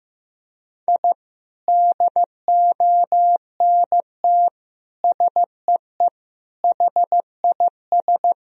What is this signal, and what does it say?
Morse code, heard as a keyed tone, about 16 wpm, 700 Hz. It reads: I DONT SEE HIS
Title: Sentences from Top 1000 Words - Review Part 1 - 15wpm